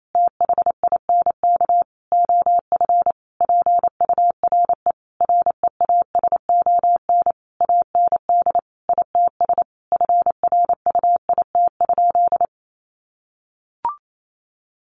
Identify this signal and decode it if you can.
Morse code, heard as a keyed tone, about 28 wpm, 700 Hz. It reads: THINK OF PURE REASON AND ITS FRUIT?
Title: Sentences from Top 1500 Words 28wpm